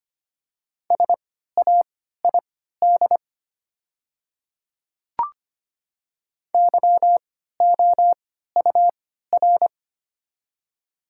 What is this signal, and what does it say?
Morse code, heard as a keyed tone, about 25 wpm, spaced as for 14 wpm, 700 Hz. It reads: SAID YOUR
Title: 3x Character Spacing - Top 500 Words - 25wpm